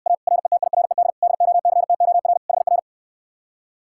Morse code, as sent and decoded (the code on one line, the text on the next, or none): .- / .-. - .. .-. -.. / -.. --- -.-. - --- .-. / .... .-.
A RETIRED DOCTOR HR